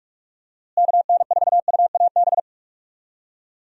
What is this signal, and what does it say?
Morse code, heard as a keyed tone, about 41 wpm, 700 Hz. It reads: KN4UAB
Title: USA Call Signs - 6 Characters - Repeated 2x - 45wpm